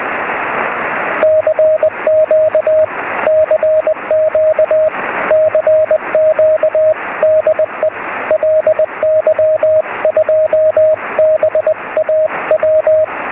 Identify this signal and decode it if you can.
Morse code, heard as a keyed tone, about 20 wpm, 620 Hz, with radio noise. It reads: CQ CQ CQ DE LY2BAW
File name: sm7lcb / 2208 ly2baw 23cm